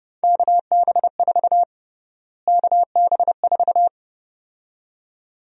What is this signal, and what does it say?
Morse code, heard as a keyed tone, about 30 wpm, 700 Hz. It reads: KB4 KB4
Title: USA Call Sign Prefixes - Repeated 2x - 30wpm